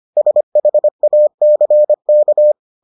Morse code, heard as a keyed tone, about 25 wpm, 600 Hz. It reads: SHACK